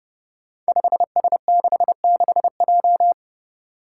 Morse code, heard as a keyed tone, about 32 wpm, 700 Hz. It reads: 5S66J
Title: Five Challenging Letter-Number Combinations 30wpm